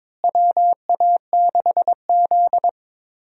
Morse code, heard as a keyed tone, about 22 wpm, 700 Hz. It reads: WA6Z
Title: USA Call Signs - 4 Characters - Repeated 3x - 22wpm